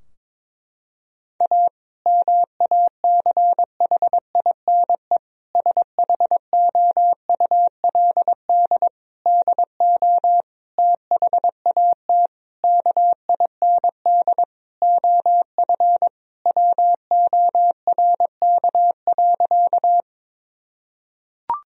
Morse code, heard as a keyed tone, about 22 wpm, 700 Hz. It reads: A MACHINE SHOULD DO THAT KIND OF WORK.
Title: Sentences from Top 1000 Words - Review Part 2 - 22wpm